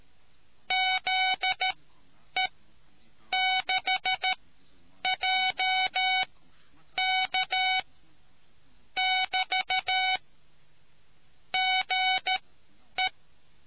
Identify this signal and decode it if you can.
Morse code, heard as a keyed tone, about 13 wpm, spaced as for 8 wpm, 750 Hz. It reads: ZE6JK <BT> GE